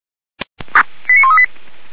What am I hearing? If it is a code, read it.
Five-tone selective call (ZVEI), heard as quick runs of five tones, tones of about 70 ms. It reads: 89138
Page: /464/SDR/ZVEI/